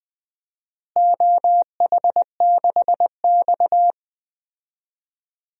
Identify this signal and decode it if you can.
Morse code, heard as a keyed tone, about 20 wpm, 700 Hz. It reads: OH6X
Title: DXCC-100 Call Signs - 4 Characters - Repeated 3x - 20wpm